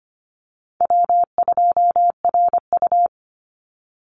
Morse code, heard as a keyed tone, about 25 wpm, 700 Hz. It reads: W2RU
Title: Easy Call Signs Repeated 2x - 25wpm